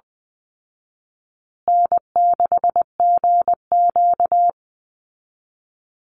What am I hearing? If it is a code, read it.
Morse code, heard as a keyed tone, about 19 wpm, 700 Hz. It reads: N6GQ
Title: Easy Call Signs Repeated 3x - 20wpm